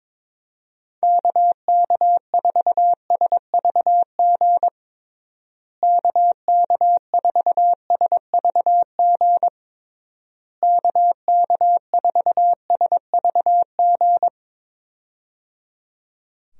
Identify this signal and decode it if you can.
Morse code, heard as a keyed tone, about 22 wpm, 700 Hz. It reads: KK4SVG KK4SVG KK4SVG
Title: USA Call Signs - 6 Characters - Repeated 3x - 22wpm